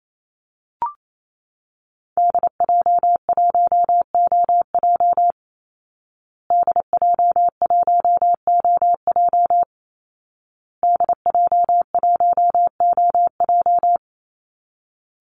Morse code, heard as a keyed tone, about 28 wpm, 700 Hz. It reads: DJ1OJ DJ1OJ DJ1OJ
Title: Call Signs Repeated 2x - 28wpm